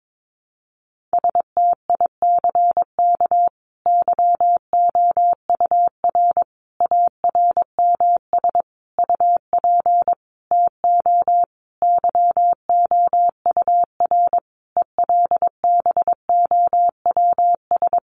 Morse code, heard as a keyed tone, about 22 wpm, 700 Hz. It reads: STICK YOUR ARMS UP TO YOUR ELBOWS
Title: A Sound Of Thunder 22wpm